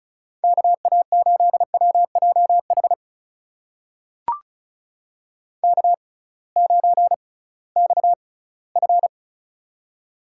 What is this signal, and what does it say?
Morse code, heard as a keyed tone, about 36 wpm, spaced as for 21 wpm, 700 Hz. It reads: KA8WJH K9XF
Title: USA Call Signs - 6x Character Spacing - 35wpm